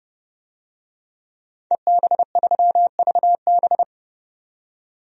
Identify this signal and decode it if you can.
Morse code, heard as a keyed tone, about 30 wpm, 700 Hz. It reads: EB3VB
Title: DXCC-100 Call Signs - 5 Characters - 30wpm